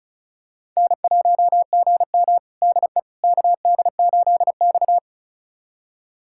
Morse code, heard as a keyed tone, about 36 wpm, 700 Hz. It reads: N1GM DE KD8X
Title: USA Callsign DE Callsign - 4 Characters - 35wpm